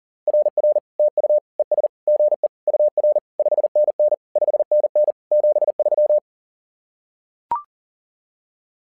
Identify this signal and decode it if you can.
Morse code, heard as a keyed tone, about 40 wpm, 600 Hz. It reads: RR TU ES GE UR 5NN 5NN 73
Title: SOTA QSO Practice 40wpm